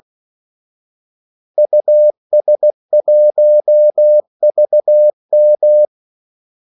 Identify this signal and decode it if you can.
Morse code, heard as a keyed tone, about 16 wpm, 600 Hz. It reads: US1VM
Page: Kaldesignaler 16 wpm